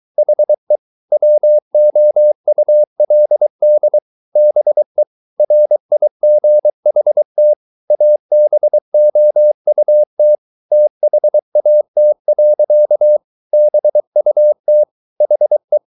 Morse code, heard as a keyed tone, about 23 wpm, 600 Hz. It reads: HE WOULD BE RIGHT ABOUT THAT. BUT HE